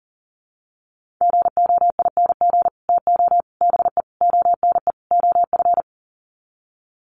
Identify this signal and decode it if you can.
Morse code, heard as a keyed tone, about 40 wpm, 700 Hz. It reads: GOING TO BE ONE OF
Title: Sets of 5 Words Encore2 40wpm